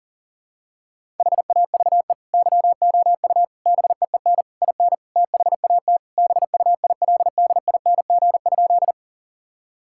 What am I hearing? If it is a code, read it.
Morse code, heard as a keyed tone, about 37 wpm, 700 Hz. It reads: HAVE YOU BEEN IN THAT BUILDING?
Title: Sentences from Top 1000 Words - Review Part 5 - 40wpm